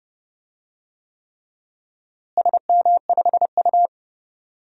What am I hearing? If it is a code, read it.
Morse code, heard as a keyed tone, about 29 wpm, 700 Hz. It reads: SM5U
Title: DXCC-100 Call Signs - 4 Characters - Repeated 3x - 30wpm